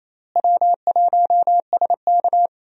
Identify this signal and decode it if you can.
Morse code, heard as a keyed tone, about 28 wpm, 700 Hz. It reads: W1SK
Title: USA Call Signs - 4 Characters - Repeated 3x - 28wpm